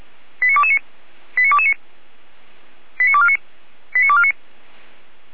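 Five-tone selective call (ZVEI), heard as quick runs of five tones, tones of about 70 ms. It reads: 89229 89229 89249 89249
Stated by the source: ZVEI five-tone selective call 89276 received by SDR, logged by the Feuerwehr Orlamünde (fire brigade) audio server.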